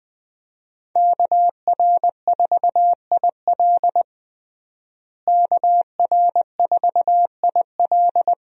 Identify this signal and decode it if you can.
Morse code, heard as a keyed tone, about 20 wpm, 700 Hz. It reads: KR4IL KR4IL
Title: USA Call Signs - 5 Characters - Repeated 2x - 20wpm